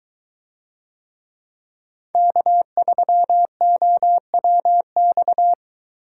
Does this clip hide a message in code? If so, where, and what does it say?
Yes: Morse code from 2.1 to 5.5 s, heard as a keyed tone, about 22 wpm, 700 Hz. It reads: K3OWX